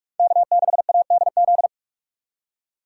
Morse code, heard as a keyed tone, about 46 wpm, 700 Hz. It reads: K6ADZ
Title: USA Call Signs - 5 Characters - 3x Character Spacing - 45wpm